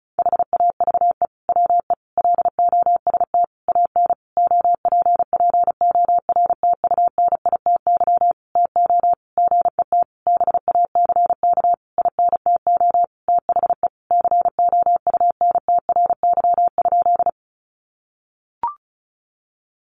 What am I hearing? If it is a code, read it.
Morse code, heard as a keyed tone, about 34 wpm, 700 Hz. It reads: HAVE WE LOST AN OPPORTUNITY TO GET BACK INTO THE COUNTRY?
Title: Sentences from Top 1400 Words 35wpm